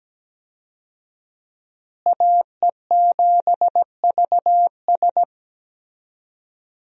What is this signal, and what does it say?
Morse code, heard as a keyed tone, about 17 wpm, 700 Hz. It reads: AE7VS
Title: USA Call Signs - 5 Characters - Repeated 2x - 17wpm